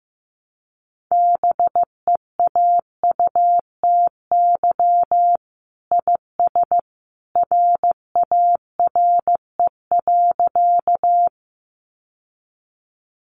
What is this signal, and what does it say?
Morse code, heard as a keyed tone, about 15 wpm, 700 Hz. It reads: BEAUTY IS RARE.
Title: Sentences from Top 1100 Words 15wpm